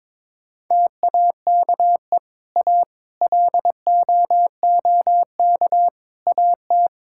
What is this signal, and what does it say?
Morse code, heard as a keyed tone, about 22 wpm, 700 Hz. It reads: TAKE A LOOK AT
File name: practice / 22wpm / Build A Phrase - 5-Word Phrases - 22wpm